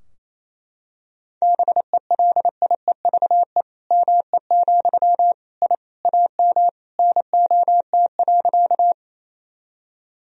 Morse code, heard as a keyed tone, about 28 wpm, 700 Hz. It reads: BELIEVE ME, I AM NOT.
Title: Sentences from Top 1000 Words - Review Part 2 - 28wpm